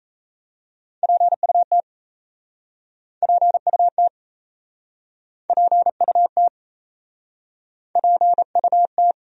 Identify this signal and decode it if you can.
Morse code, heard as a keyed tone, about 37 wpm, 700 Hz. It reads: PUT PUT PUT PUT